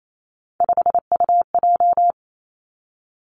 Morse code, heard as a keyed tone, about 28 wpm, 700 Hz. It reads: 5UJ
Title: Three Challenging Letter-Number Combinations 28wpm